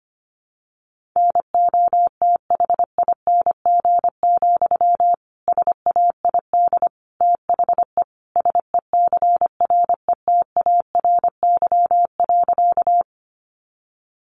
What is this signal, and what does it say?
Morse code, heard as a keyed tone, about 25 wpm, 700 Hz. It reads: NOTHING, SAID THE SECRETARY.
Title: Sentences from Top 1300 Words 25wpm